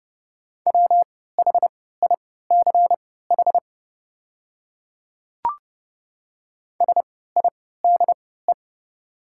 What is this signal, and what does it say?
Morse code, heard as a keyed tone, about 30 wpm, spaced as for 17 wpm, 700 Hz. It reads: WHICH SIDE
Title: 3x Character Spacing - Top 100 Words - 30wpm